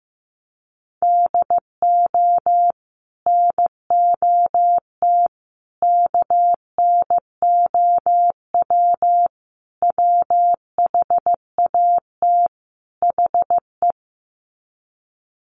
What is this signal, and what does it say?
Morse code, heard as a keyed tone, about 15 wpm, 700 Hz. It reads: DO NOT KNOW WHAT HE
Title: Sets of 5 Words Encore2 15wpm